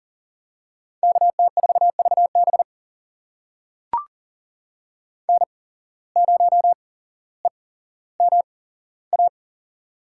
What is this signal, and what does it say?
Morse code, heard as a keyed tone, about 40 wpm, spaced as for 17 wpm, 700 Hz. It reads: KT4VB N0EMA
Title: USA Call Signs - 5 Characters - 8x Character Spacing - 40wpm